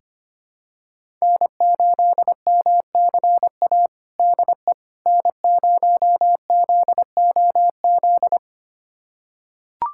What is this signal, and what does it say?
Morse code, heard as a keyed tone, about 25 wpm, 700 Hz. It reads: N8MCA DE N0ZOZ
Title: USA Callsign DE Callsign - 5 Characters - 25wpm